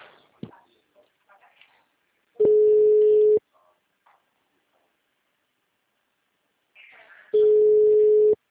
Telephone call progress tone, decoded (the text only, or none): ringback tone